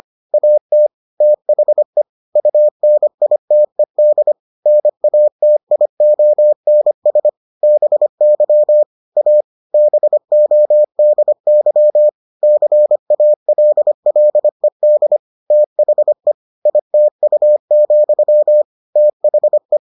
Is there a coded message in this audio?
Yes: Morse code from 0.3 to 19.8 s, heard as a keyed tone, about 25 wpm, 600 Hz. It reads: AT THE UNITED NATIONS BY A BODY CALLED THE ITU, THE